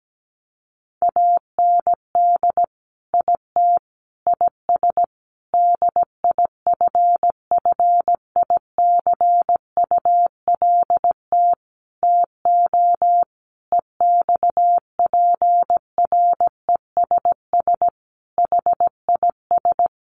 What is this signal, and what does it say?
Morse code, heard as a keyed tone, about 17 wpm, 700 Hz. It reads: AND IT IS DIFFICULT TO EXPRESS HIS
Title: Sentences from Top 1100 Words 17wpm